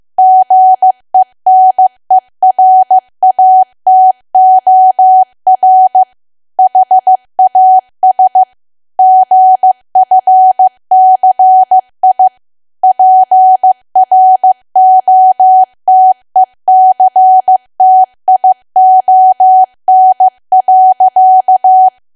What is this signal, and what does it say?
Morse code, heard as a keyed tone, about 15 wpm, 750 Hz. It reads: GENERATOR HAS GFCI PROTECTION.
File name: cwa / W1AW-Level2 / 15WPM